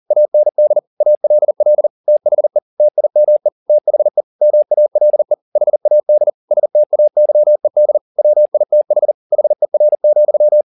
Morse code, heard as a keyed tone, about 40 wpm, 600 Hz. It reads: AND ALL THE TIME THE MALE HAD STAYED WITH HER,